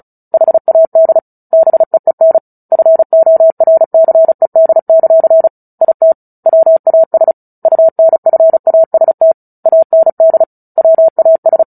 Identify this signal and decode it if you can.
Morse code, heard as a keyed tone, about 36 wpm, 650 Hz. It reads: HAD BEEN FORCED; IT WAS UNFAST AND WAS
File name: Book Two / 09-Chapter 9 - 35 WPM